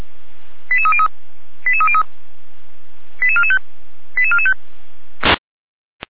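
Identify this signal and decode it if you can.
Five-tone selective call (ZVEI), heard as quick runs of five tones, tones of about 70 ms. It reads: 88303 88303 88445 88445
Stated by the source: ZVEI five-tone selective call received on SDR.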